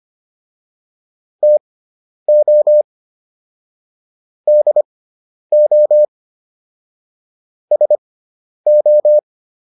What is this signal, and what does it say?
Morse code, heard as a keyed tone, about 25 wpm, spaced as for 10 wpm, 600 Hz. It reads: TO DO SO